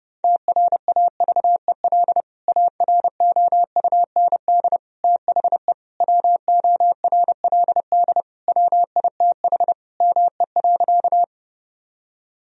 Morse code, heard as a keyed tone, about 32 wpm, 700 Hz. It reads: TRAVEL AROUND THE WORLD WITH ME.
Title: Sentences from Top 1000 Words - Review Part 3 - 30wpm